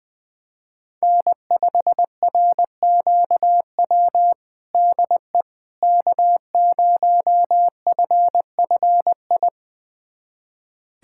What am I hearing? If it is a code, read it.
Morse code, heard as a keyed tone, about 20 wpm, 700 Hz. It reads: N5RQW DE K0FFI
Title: USA Callsign DE Callsign 20wpm